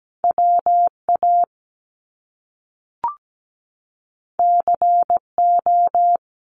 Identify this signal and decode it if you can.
Morse code, heard as a keyed tone, about 17 wpm, 700 Hz. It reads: WA CO
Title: US States 17wpm